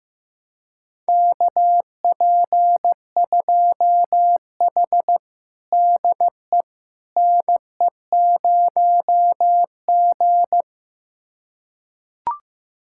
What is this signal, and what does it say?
Morse code, heard as a keyed tone, about 16 wpm, 700 Hz. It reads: KP2H DE NE0G
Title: USA Callsign DE Callsign - 4 Characters - 15wpm